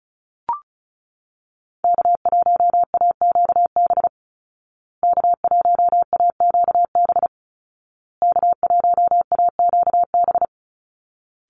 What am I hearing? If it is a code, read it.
Morse code, heard as a keyed tone, about 34 wpm, 700 Hz. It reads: K1AQB K1AQB K1AQB
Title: Call Signs Repeated 2x - 35wpm